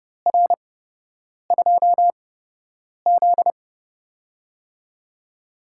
Morse code, heard as a keyed tone, about 31 wpm, spaced as for 8 wpm, 700 Hz. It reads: R2Z
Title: USA Call Signs - 4 Characters - 8x Character Spacing - 30wpm